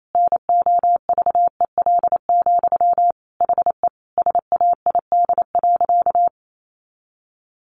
Morse code, heard as a keyed tone, about 28 wpm, 700 Hz. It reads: NOVEL, HE SAID.
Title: Sentences from Top 1300 Words 28wpm